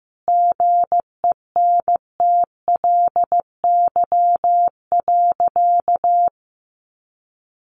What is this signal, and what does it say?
Morse code, heard as a keyed tone, about 15 wpm, 700 Hz. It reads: GENTLY.
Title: Sentences from Top 1500 Words 15wpm